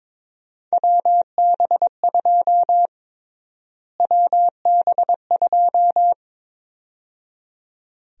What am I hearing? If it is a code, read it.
Morse code, heard as a keyed tone, about 21 wpm, 700 Hz. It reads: WB2 WB2
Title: USA Call Sign Prefixes - Repeated 2x - 22wpm